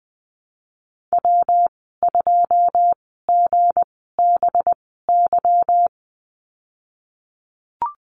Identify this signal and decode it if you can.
Morse code, heard as a keyed tone, about 20 wpm, spaced as for 15 wpm, 700 Hz. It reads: W2GBY
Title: 186 Lesson 30g - Letter b - Call Signs 20wpm